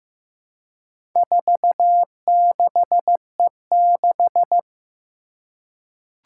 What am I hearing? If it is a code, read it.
Morse code, heard as a keyed tone, about 15 wpm, 700 Hz. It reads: 46E6
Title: Four Challenging Letter-Number Combinations 15wpm